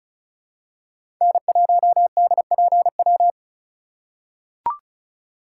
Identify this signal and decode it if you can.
Morse code, heard as a keyed tone, about 36 wpm, 700 Hz. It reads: N1DPW
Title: USA Call Signs - 5 Characters - 4x Character Spacing - 35wpm